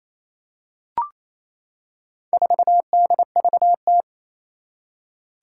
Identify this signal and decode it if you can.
Morse code, heard as a keyed tone, about 29 wpm, 700 Hz. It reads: 4DVT
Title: Four Challenging Letter-Number Combinations 28wpm